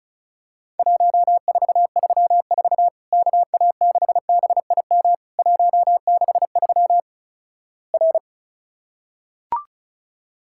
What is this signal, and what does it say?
Morse code, heard as a keyed tone, about 36 wpm, 700 Hz. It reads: 1434 KA6BIM 163 R
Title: Worked All Europe DX Contest - QTC Exchange - 35wpm